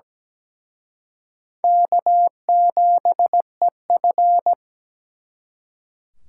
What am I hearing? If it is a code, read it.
Morse code, heard as a keyed tone, about 20 wpm, 700 Hz. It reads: K7EF
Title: Speed Racing - USA 1x2 Call Signs 17wpm